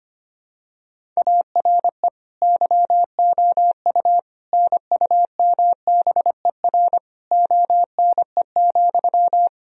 Morse code, heard as a keyed tone, about 25 wpm, 700 Hz. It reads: ARE YOU NUMBER ONE,
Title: Sentences from Top 1000 Words - Review Part 1 - 25wpm